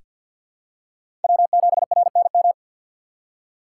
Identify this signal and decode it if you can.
Morse code, heard as a keyed tone, about 49 wpm, 700 Hz. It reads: W7RNM
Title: USA Call Signs - 5 Characters - 2x Character Spacing - 50wpm